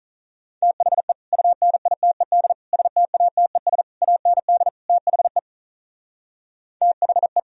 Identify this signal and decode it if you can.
Morse code, heard as a keyed tone, about 46 wpm, 700 Hz. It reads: THE UNITED STATES AND THE THE